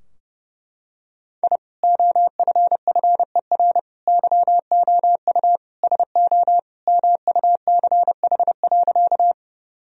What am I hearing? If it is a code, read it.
Morse code, heard as a keyed tone, about 30 wpm, 700 Hz. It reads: I OFFER YOU SO MUCH.
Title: Sentences from Top 1000 Words - Review Part 4 - 30wpm